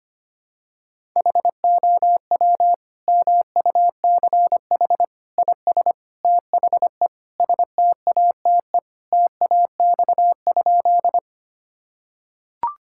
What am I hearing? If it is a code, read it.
Morse code, heard as a keyed tone, about 25 wpm, 700 Hz. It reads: HOW MUCH IS THE STATE TAX?
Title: Sentences from Top 1000 Words - Review Part 5 - 25wpm